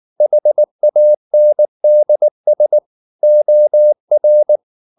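Morse code, heard as a keyed tone, about 19 wpm, 600 Hz. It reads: HANDS OR
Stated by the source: Morse code at 19 wpm from CW med Gnister, reading Ken Follett's novel Never in English.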